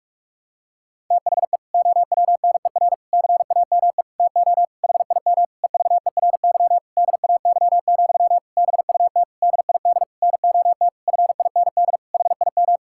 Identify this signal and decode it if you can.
Morse code, heard as a keyed tone, about 42 wpm, 700 Hz. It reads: THE OWNER CAME TO HIM EVERY DAY, BUT DID NOT FIND HIM
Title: Sentences from Top 1600 Words 45wpm